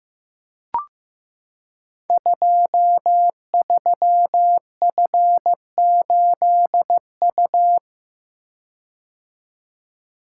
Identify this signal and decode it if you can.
Morse code, heard as a keyed tone, about 15 wpm, 700 Hz. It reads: 23F8U
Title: Code Groups - 5 Letter:Numbers - 15wpm